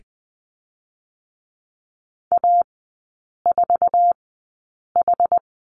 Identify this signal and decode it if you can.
Morse code, heard as a keyed tone, about 20 wpm, spaced as for 8 wpm, 690 Hz. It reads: A4H
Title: 060 Lesson 11e - Letter h - Three Characters 20wpm